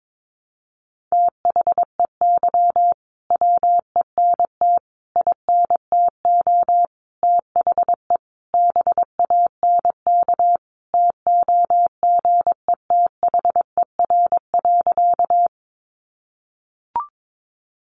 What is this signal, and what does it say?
Morse code, heard as a keyed tone, about 22 wpm, 700 Hz. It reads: THEY WENT INTO THE BANK TOGETHER.
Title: Sentences from Top 900 Words 22wpm